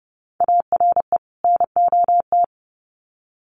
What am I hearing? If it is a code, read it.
Morse code, heard as a keyed tone, about 30 wpm, 700 Hz. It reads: ARE NOT
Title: Sets of 2 Words 30wpm